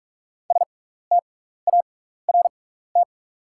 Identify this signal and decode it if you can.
Morse code, heard as a keyed tone, about 46 wpm, spaced as for 15 wpm, 690 Hz. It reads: START
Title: 6x Character Spacing - Top 300 Words - 45wpm